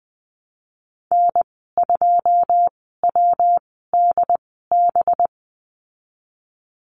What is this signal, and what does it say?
Morse code, heard as a keyed tone, about 20 wpm, spaced as for 15 wpm, 700 Hz. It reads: N2WDB
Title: 193 Lesson 31g - Letter v - Call Signs 20wpm